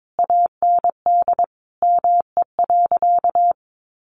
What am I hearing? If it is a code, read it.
Morse code, heard as a keyed tone, about 22 wpm, 700 Hz. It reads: AND ME.
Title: Sentences from Top 500 Words 22wpm